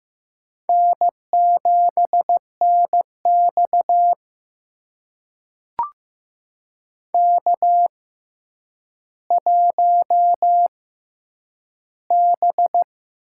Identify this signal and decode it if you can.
Morse code, heard as a keyed tone, about 15 wpm, spaced as for 8 wpm, 700 Hz. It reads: N7NX K1B
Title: USA Call Signs - 6x Character Spacing - 15wpm